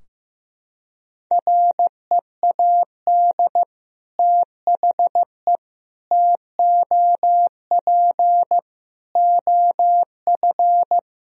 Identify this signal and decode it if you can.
Morse code, heard as a keyed tone, about 15 wpm, 700 Hz. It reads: READ THE TOP OF
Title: Sentences from Top 1000 Words - Review Part 2 - 15wpm